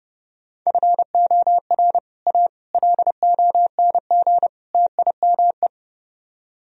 Morse code, heard as a keyed tone, about 30 wpm, 700 Hz. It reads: FOR A LONG TIME